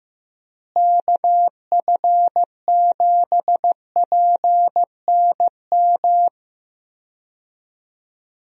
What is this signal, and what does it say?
Morse code, heard as a keyed tone, about 15 wpm, 700 Hz. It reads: KF7PNM
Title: USA Call Signs - 6 Characters - Repeated 2x - 15wpm